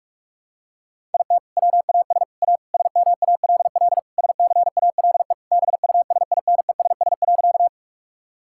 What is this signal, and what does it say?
Morse code, heard as a keyed tone, about 46 wpm, 700 Hz. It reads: IT WAS A SMALL SCALE BUSINESS.
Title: Sentences from Top 1000 Words - Review Part 5 - 45wpm